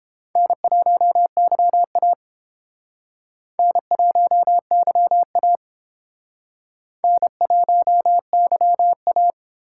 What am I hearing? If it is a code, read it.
Morse code, heard as a keyed tone, about 29 wpm, 700 Hz. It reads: N1YA N1YA N1YA